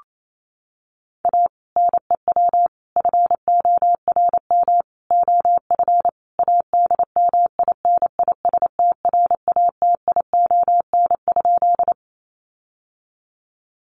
Morse code, heard as a keyed tone, about 28 wpm, 700 Hz. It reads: A NEW FORM OF ADMINISTRATION?
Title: Sentences from Top 1300 Words 28wpm